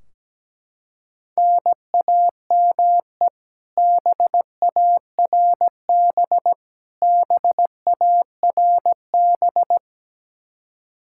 Morse code, heard as a keyed tone, about 17 wpm, 700 Hz. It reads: NAME BARB BARB
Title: Practice QSO - Exchange 1 - Beginning - 17wpm